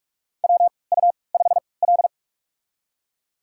Morse code, heard as a keyed tone, about 46 wpm, spaced as for 26 wpm, 700 Hz. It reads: WU5L
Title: USA Call Signs - 4 Characters - 3x Character Spacing - 45wpm